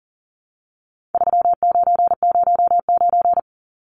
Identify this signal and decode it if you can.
Morse code, heard as a keyed tone, about 38 wpm, 700 Hz. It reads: 3909
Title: Numbers - 4-Digit 40wpm